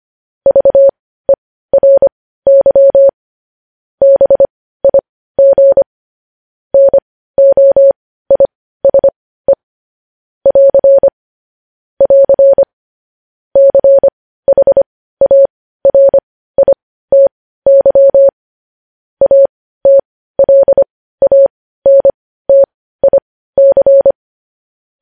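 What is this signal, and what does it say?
Morse code, heard as a keyed tone, about 25 wpm, spaced as for 15 wpm, 550 Hz. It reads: VERY BIG NOISE <AR> <AR> CHARITY ATLANTIC
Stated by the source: News headlines sent in Morse code at 15 wpm.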